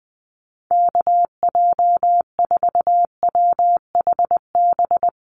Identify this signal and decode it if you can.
Morse code, heard as a keyed tone, about 20 wpm, 700 Hz. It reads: KJ4WHB
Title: USA Call Signs 20wpm